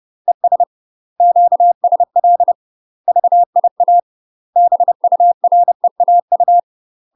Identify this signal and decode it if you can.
Morse code, heard as a keyed tone, about 30 wpm, 700 Hz. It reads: ES QSL VIA BUREAU
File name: UK MASTER QSOs TRAINING / 30WPM / UK MASTER QSOs 0020